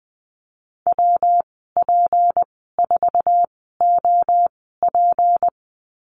Morse code, heard as a keyed tone, about 20 wpm, spaced as for 15 wpm, 700 Hz. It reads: WP4OP
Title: 154 Lesson 25g - Letter p - Call Signs 20wpm